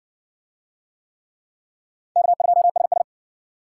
Morse code, heard as a keyed tone, about 60 wpm, 700 Hz. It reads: K2SS